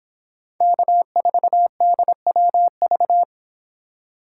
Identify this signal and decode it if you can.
Morse code, heard as a keyed tone, about 30 wpm, 700 Hz. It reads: K4DWV